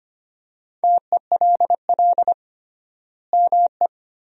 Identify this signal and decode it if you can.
Morse code, heard as a keyed tone, about 25 wpm, 700 Hz. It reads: TELL ME
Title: 3x Word-Spacing - Sentences from Top 200 Words - 25wpm